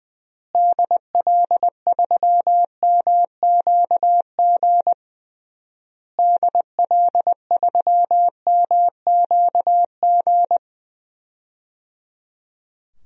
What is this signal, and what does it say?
Morse code, heard as a keyed tone, about 20 wpm, 700 Hz. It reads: DL3MQG DL3MQG
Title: DXCC-100 Call Signs - 6 Characters - Repeated 3x - 20wpm